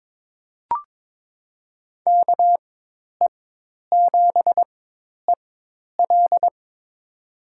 Morse code, heard as a keyed tone, about 22 wpm, spaced as for 10 wpm, 700 Hz. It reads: KE7EL
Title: USA Call Signs - 5 Characters - 4x Character Spacing - 22wpm